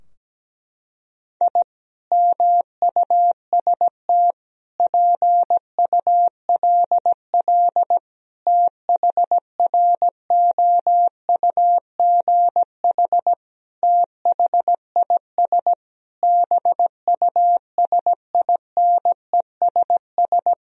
Morse code, heard as a keyed tone, about 17 wpm, 700 Hz. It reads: I MUST PULL THROUGH THIS BUSINESS
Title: Sentences from Top 1000 Words - Review Part 2 - 17wpm